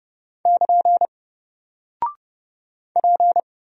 Morse code, heard as a keyed tone, about 30 wpm, 700 Hz. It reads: <KN> P
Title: Single Character (Letter:Number:Punctuation:Prosign) Alt 30wpm